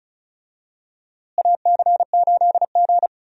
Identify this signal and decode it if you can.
Morse code, heard as a keyed tone, about 36 wpm, 700 Hz. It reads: AC8G
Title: USA Call Signs - 4 Characters - Repeated 2x - 35wpm